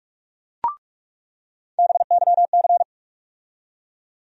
Morse code, heard as a keyed tone, about 47 wpm, 700 Hz. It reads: BYC
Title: USA Call Sign Postfixes - 3 Letters 45wpm